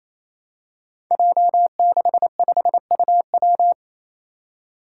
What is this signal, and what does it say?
Morse code, heard as a keyed tone, about 28 wpm, 700 Hz. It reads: J65UW
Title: Five Challenging Letter-Number Combinations 28wpm